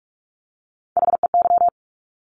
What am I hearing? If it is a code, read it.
Morse code, heard as a keyed tone, about 46 wpm, 690 Hz. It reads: HEY
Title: Top Three Letter Combinations in Words 45wpm